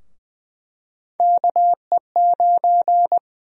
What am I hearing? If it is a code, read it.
Morse code, heard as a keyed tone, about 20 wpm, 700 Hz. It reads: KE9
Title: USA Call Sign Prefixes - 20wpm